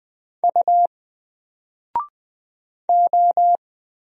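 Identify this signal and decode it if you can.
Morse code, heard as a keyed tone, about 20 wpm, 700 Hz. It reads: U O
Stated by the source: Single letter and number Morse drill at 20 wpm, from N4LCD.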